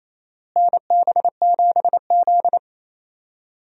Morse code, heard as a keyed tone, about 28 wpm, 700 Hz. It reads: NB7Z
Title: USA Call Signs - 4 Characters - Repeated 2x - 28wpm